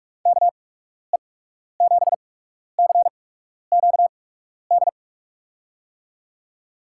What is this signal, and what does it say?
Morse code, heard as a keyed tone, about 42 wpm, spaced as for 12 wpm, 700 Hz. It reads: KE7CQD
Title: USA Call Signs - 6 Characters - 8x Character Spacing - 45wpm